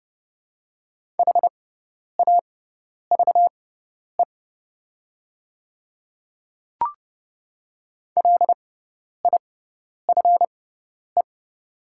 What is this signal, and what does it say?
Morse code, heard as a keyed tone, about 30 wpm, spaced as for 10 wpm, 690 Hz. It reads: HAVE LIFE
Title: 6x Character Spacing - Top 500 Words - 30wpm